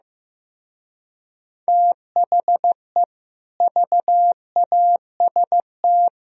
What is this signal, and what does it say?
Morse code, heard as a keyed tone, about 15 wpm, 700 Hz. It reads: THE VAST
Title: Speed-Racing - Sets of 4 Words 15wpm